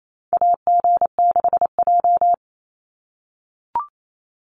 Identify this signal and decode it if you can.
Morse code, heard as a keyed tone, about 28 wpm, 700 Hz. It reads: AG6J
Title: USA Call Signs 28wpm